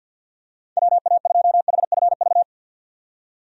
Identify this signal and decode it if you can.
Morse code, heard as a keyed tone, about 52 wpm, 700 Hz. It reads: WA2HFV
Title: USA Call Signs - 6 Characters - 50wpm